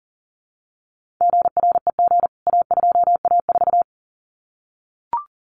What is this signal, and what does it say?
Morse code, heard as a keyed tone, about 40 wpm, 700 Hz. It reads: GREG A2A4
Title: CWT Exchange 40wpm